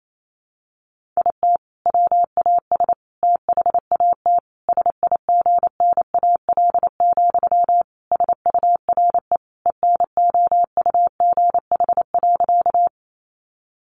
Morse code, heard as a keyed tone, about 28 wpm, 700 Hz. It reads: IT WAS THAT SIGNAL, SURE ENOUGH.
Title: Sentences from Top 1400 Words 28wpm